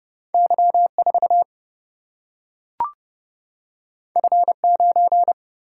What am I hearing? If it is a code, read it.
Morse code, heard as a keyed tone, about 29 wpm, 700 Hz. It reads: Y4 F9
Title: Single-Letter Single-Number 30wpm